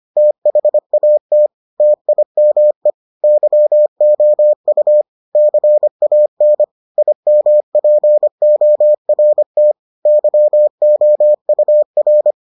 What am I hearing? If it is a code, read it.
Morse code, heard as a keyed tone, about 25 wpm, 600 Hz. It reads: THAT TIME YOU CAN IMPORT YOUR